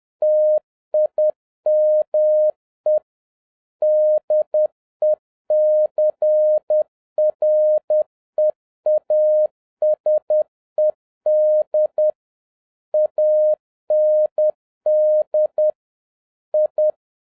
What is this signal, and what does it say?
Morse code, heard as a keyed tone, about 10 wpm, 610 Hz. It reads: TIME DECREASED AND I